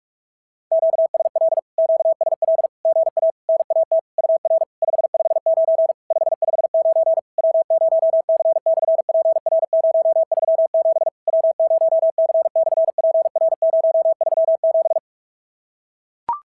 Morse code, heard as a keyed tone, about 44 wpm, 660 Hz. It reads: QSL QSL GA NAT UR 559 559 W0C/PR037 W0C/PR037
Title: SOTA QSO Practice 45wpm